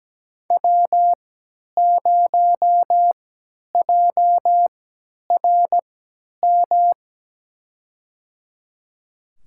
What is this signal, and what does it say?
Morse code, heard as a keyed tone, about 17 wpm, spaced as for 10 wpm, 700 Hz. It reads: W0JRM
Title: USA Call Signs - 5 Characters - 3x Character Spacing - 17wpm